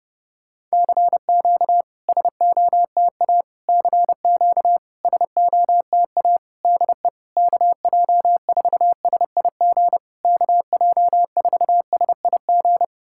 Morse code, heard as a keyed tone, about 31 wpm, 690 Hz. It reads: CQ SOTA CQ SOTA DE KJ4SIG KJ4SIG
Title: SOTA QSO Practice 30wpm